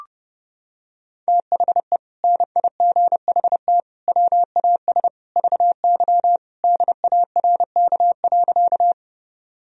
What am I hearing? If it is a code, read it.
Morse code, heard as a keyed tone, about 28 wpm, 700 Hz. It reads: THE NIGHT WAS VY DARK.
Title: Sentences from Top 500 Words and Abbreviations 30wpm